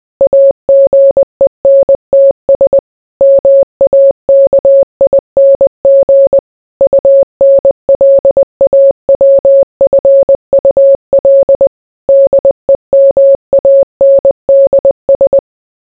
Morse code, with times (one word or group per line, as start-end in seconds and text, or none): 0.2-2.8 AGENTS
3.2-6.4 MAKING
6.8-11.7 UNLAWFUL
12.1-15.4 DEMANDS